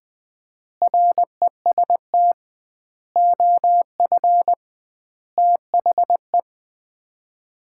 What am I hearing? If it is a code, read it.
Morse code, heard as a keyed tone, about 20 wpm, 700 Hz. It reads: REST OF THE
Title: 2x Word-Spacing - Sets of 5 Words - 20wpm